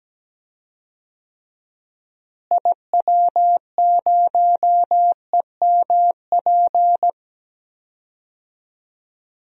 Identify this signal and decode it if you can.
Morse code, heard as a keyed tone, about 17 wpm, 700 Hz. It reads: IW0EMP
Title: DXCC-100 Call Signs - 6 Characters - Repeated 3x - 17wpm